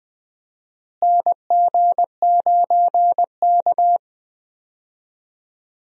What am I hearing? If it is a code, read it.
Morse code, heard as a keyed tone, about 21 wpm, 700 Hz. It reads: NG9K
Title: USA Call Signs - 4 Characters - Repeated 2x - 20wpm